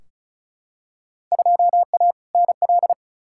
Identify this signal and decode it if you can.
Morse code, heard as a keyed tone, about 36 wpm, 700 Hz. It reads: 2A NL
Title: Field Day Exchanges Encore - 35wpm